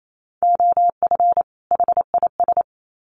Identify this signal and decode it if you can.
Morse code, heard as a keyed tone, about 28 wpm, 700 Hz. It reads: OF HIS
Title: Sets of 2 Words 28wpm